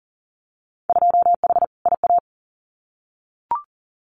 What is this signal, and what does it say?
Morse code, heard as a keyed tone, about 40 wpm, 700 Hz. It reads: 2H IA
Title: Winter Field Day Exchanges Encore 40wpm